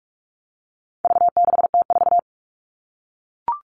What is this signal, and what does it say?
Morse code, heard as a keyed tone, about 44 wpm, 700 Hz. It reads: V6T4
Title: All Cut Numbers - 4-Digit 45wpm